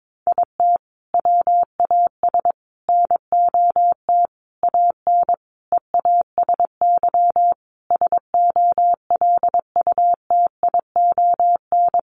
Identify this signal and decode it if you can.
Morse code, heard as a keyed tone, about 22 wpm, 700 Hz. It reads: IT WAS NOT AN EASY SOLUTION
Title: Sentences from Top 1300 Words 22wpm